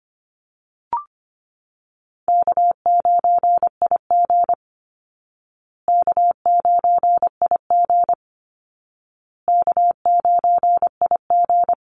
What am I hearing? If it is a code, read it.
Morse code, heard as a keyed tone, about 25 wpm, 700 Hz. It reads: K9IG K9IG K9IG
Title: Call Signs Repeated 2x - 25wpm